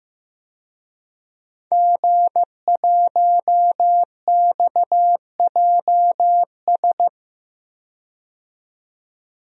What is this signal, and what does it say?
Morse code, heard as a keyed tone, about 15 wpm, 700 Hz. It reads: G1XJS
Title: DXCC-100 Call Signs - 5 Characters - 15wpm